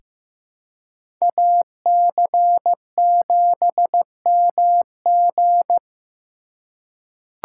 Morse code, heard as a keyed tone, about 15 wpm, 700 Hz. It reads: AC7MG
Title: USA Call Signs - 5 Characters - 2x Character Spacing - 15wpm